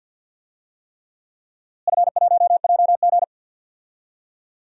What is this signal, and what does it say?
Morse code, heard as a keyed tone, about 49 wpm, 700 Hz. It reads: F1JG
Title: DXCC-100 Call Signs - 4 Characters - Repeated 3x - 50wpm